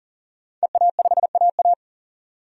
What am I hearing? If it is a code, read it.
Morse code, heard as a keyed tone, about 40 wpm, 700 Hz. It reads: EA5AA
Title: DXCC-100 Call Signs - 5 Characters - 40wpm